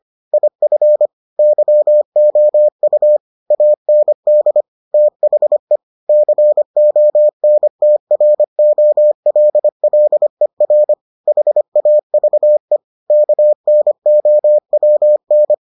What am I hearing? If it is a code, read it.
Morse code, heard as a keyed tone, about 25 wpm, 600 Hz. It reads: IF YOU AND THE CONTROLLER HAVE KNOWN